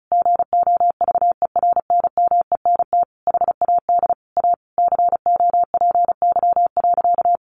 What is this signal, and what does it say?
Morse code, heard as a keyed tone, about 34 wpm, 700 Hz. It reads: GOVERNMENT HAD A COPY.
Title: Sentences from Top 1400 Words 35wpm